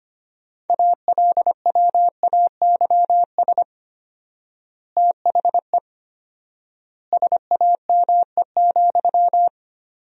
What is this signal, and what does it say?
Morse code, heard as a keyed tone, about 25 wpm, 700 Hz. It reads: ALWAYS THE SAME,
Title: 4x Word-Spacing - Sentences from Top 300 Words - 25wpm